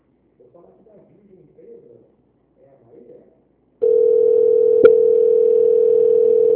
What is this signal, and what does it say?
Ringback tone (North America), 440 + 480 Hz, with speech.
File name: out- -2006- - - .120